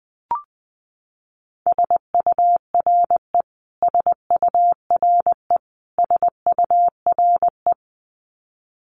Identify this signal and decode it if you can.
Morse code, heard as a keyed tone, about 19 wpm, 700 Hz. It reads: SURE SURE SURE
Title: Top 300 Words Repeated 20wpm